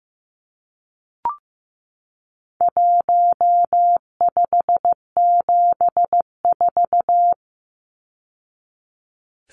Morse code, heard as a keyed tone, about 15 wpm, 700 Hz. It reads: 1574
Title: Numbers - 4-Digit 15wpm